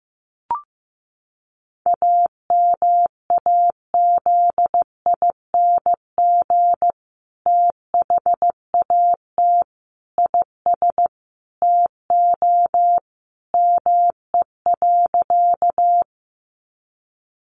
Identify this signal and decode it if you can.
Morse code, heard as a keyed tone, about 15 wpm, 700 Hz. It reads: AMAZING THAT IS TO ME.
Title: Sentences from Top 1500 Words 15wpm